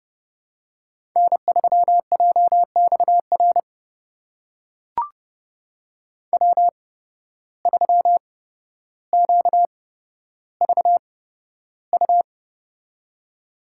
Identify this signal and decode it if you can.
Morse code, heard as a keyed tone, about 30 wpm, spaced as for 13 wpm, 700 Hz. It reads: N3JXR W3QVU
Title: USA Call Signs - 5 Characters - 8x Character Spacing - 30wpm